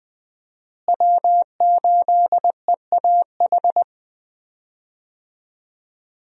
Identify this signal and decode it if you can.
Morse code, heard as a keyed tone, about 21 wpm, 700 Hz. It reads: W8EAH
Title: USA Call Signs - 5 Characters - Repeated 2x - 20wpm